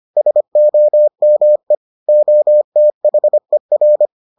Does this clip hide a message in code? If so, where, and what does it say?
Yes: Morse code from 0.2 to 4.1 s, heard as a keyed tone, about 25 wpm, 600 Hz. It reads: SOME OTHER